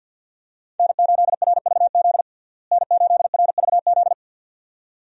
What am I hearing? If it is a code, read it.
Morse code, heard as a keyed tone, about 52 wpm, 700 Hz. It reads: N8RVZ N8RVZ
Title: USA Call Signs - 5 Characters - Repeated 2x - 50wpm